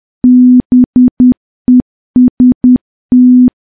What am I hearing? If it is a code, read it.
Morse code, heard as a keyed tone, about 10 wpm, 250 Hz. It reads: BEST
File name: morse / morse-training-10WPM